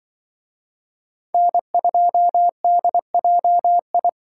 Morse code, heard as a keyed tone, about 24 wpm, 700 Hz. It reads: N2DJI
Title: Speed Racing - USA Call Signs 20wpm